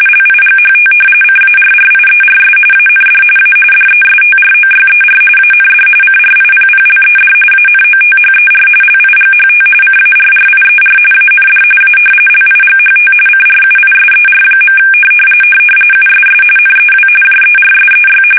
RTTY, about 45.45 baud. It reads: LIGHT CARRIERS AND A DESTROYER AND CRIPPLED ANOTHER LIGHT CARRIER AND A LIGHT CRUISER. USS NEW JERSEY
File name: fsk850 / battleshipnj